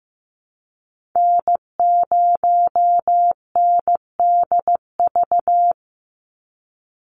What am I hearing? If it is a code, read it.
Morse code, heard as a keyed tone, about 15 wpm, 700 Hz. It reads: N0NDV
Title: USA Call Signs 15wpm